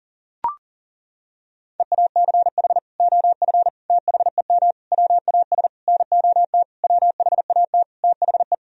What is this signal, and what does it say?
Morse code, heard as a keyed tone, about 40 wpm, 700 Hz. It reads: EACH OF THEM WAS NOT WHAT THE
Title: Sentences from Top 1000 Words - Review Part 1 - 40wpm